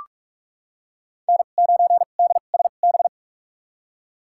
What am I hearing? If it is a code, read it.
Morse code, heard as a keyed tone, about 46 wpm, spaced as for 32 wpm, 700 Hz. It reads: N9DSB
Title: USA Call Signs - 5 Characters - 2x Character Spacing - 45wpm